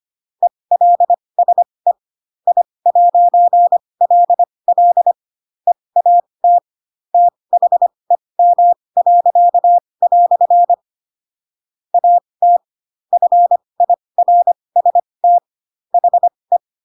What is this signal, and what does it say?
Morse code, heard as a keyed tone, about 25 wpm, spaced as for 20 wpm, 700 Hz. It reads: ELSE I'LL EAT THEM." AT FIRST HE
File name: Pinocchio / Pinocchio ch7 15 25 ik1hge